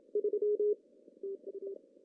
Morse code, heard as a keyed tone, about 28 wpm, 410 Hz. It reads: 3 TU
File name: CW / cqww-cw-k5tr